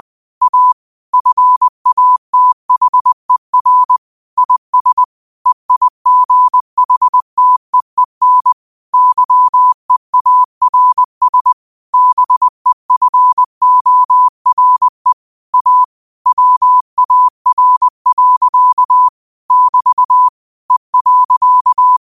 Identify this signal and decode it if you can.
Morse code, heard as a keyed tone, about 20 wpm, 1000 Hz. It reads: A FATHER IS EIGHTEEN YEARS BEFORE A WAR. <BT> E.
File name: qotdfiles / 20wpm